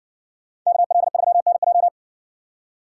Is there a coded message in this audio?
Yes: Morse code from 0.7 to 1.9 s, heard as a keyed tone, about 57 wpm, 700 Hz. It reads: KL3NJ